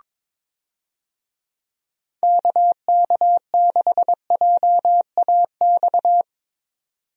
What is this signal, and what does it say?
Morse code, heard as a keyed tone, about 22 wpm, 700 Hz. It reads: KK6JAX
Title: USA Call Signs - 6 Characters - 22wpm